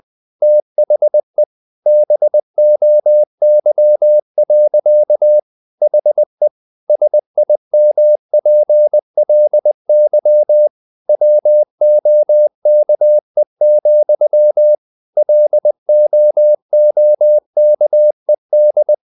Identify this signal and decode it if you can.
Morse code, heard as a keyed tone, about 20 wpm, 600 Hz. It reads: THE BOY. HE SIMPLY WOKE, LOOKED
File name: the old man and the sea / the old man and the sea part 1 20wpm